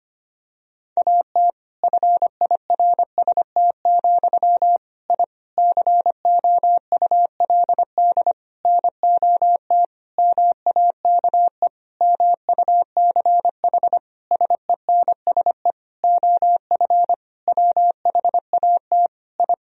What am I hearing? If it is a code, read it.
Morse code, heard as a keyed tone, about 25 wpm, 700 Hz. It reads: AT FIRST, I COULD NOT MAKE MUCH SENSE OF WHAT I
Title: Sentences from Top 1000 Words - Review Part 4 - 25wpm